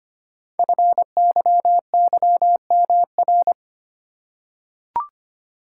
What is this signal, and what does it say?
Morse code, heard as a keyed tone, about 25 wpm, 700 Hz. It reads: FYYMR
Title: Code Groups - 5 Letter:Numbers - 25wpm